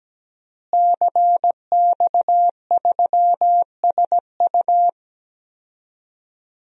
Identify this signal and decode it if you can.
Morse code, heard as a keyed tone, about 17 wpm, 700 Hz. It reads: CX3SU
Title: DXCC-100 Call Signs - 5 Characters - Repeated 3x - 17wpm